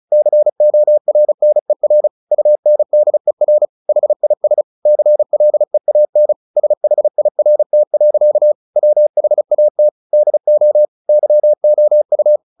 Morse code, heard as a keyed tone, about 34 wpm, 600 Hz. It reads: CORNER UNDER HIS CLEAN SHIRT. WHAT DO YOU
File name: the old man and the sea / the old man and the sea 1 35wpm